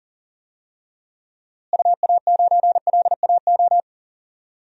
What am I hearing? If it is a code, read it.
Morse code, heard as a keyed tone, about 40 wpm, 700 Hz. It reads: UA9LAO
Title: DXCC-100 Call Signs - 6 Characters - Repeated 2x - 40wpm